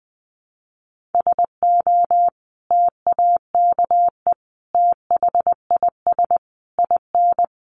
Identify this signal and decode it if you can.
Morse code, heard as a keyed tone, about 20 wpm, 700 Hz. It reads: SO TAKE THIS IN
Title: Sentences from Top 1500 Words 20wpm